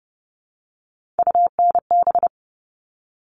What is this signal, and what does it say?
Morse code, heard as a keyed tone, about 30 wpm, 700 Hz. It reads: UNB
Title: All Cut Numbers - 3-Digit 30wpm